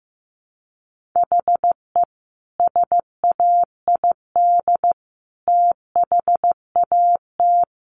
Morse code, heard as a keyed tone, about 15 wpm, 700 Hz. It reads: HE SAID THAT
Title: Sentences from Top 1000 Words 15wpm